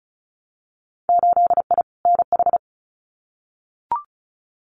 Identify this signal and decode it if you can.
Morse code, heard as a keyed tone, about 34 wpm, 700 Hz. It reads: 8I NH
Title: Winter Field Day Exchanges Encore 35wpm